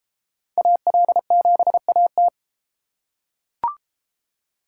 Morse code, heard as a keyed tone, about 31 wpm, 700 Hz. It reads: AL7AT